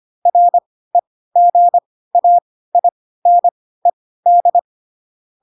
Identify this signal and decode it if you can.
Morse code, heard as a keyed tone, about 25 wpm, spaced as for 16 wpm, 700 Hz. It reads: REGAINED
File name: Pinocchio / Pinocchio ch1 10 25 ik1hge